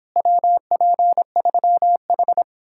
Morse code, heard as a keyed tone, about 31 wpm, 700 Hz. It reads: WP3H